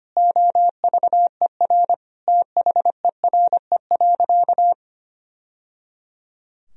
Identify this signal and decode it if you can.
Morse code, heard as a keyed tone, about 25 wpm, 700 Hz. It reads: OVER THERE.
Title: Sentences from Top 1000 Words - Review Part 2 - 25wpm